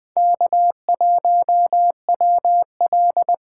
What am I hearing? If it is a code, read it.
Morse code, heard as a keyed tone, about 21 wpm, 700 Hz. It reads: K1WL
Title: Speed Racing - USA Call Signs 17wpm